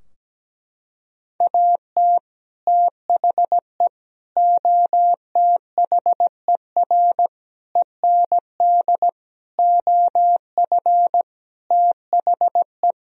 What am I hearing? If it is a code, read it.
Morse code, heard as a keyed tone, about 17 wpm, 700 Hz. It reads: AT THE OTHER END OF THE
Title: Sentences from Top 1000 Words - Review Part 2 - 17wpm